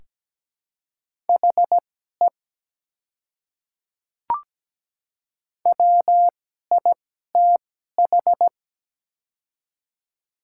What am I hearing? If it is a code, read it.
Morse code, heard as a keyed tone, about 17 wpm, spaced as for 12 wpm, 700 Hz. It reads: HE WITH
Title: 2x Character Spacing - Top 100 Words - 17wpm